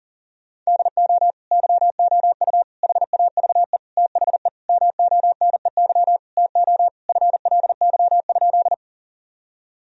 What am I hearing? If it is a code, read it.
Morse code, heard as a keyed tone, about 40 wpm, 700 Hz. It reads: DO YOU HAVE THE MONEY TO FLY?
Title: Sentences from Top 1000 Words - Review Part 2 - 40wpm